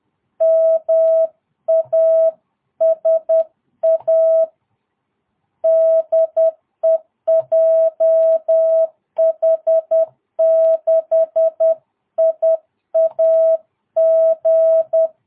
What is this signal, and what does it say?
Morse code, heard as a keyed tone, about 10 wpm, 650 Hz. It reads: MASA DE JH6IAG
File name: jh8cbh / 2022cw